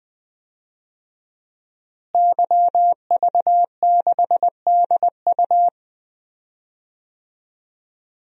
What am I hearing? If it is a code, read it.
Morse code, heard as a keyed tone, about 20 wpm, 700 Hz. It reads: YV6DU
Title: DXCC-100 Call Signs - 5 Characters - Repeated 3x - 20wpm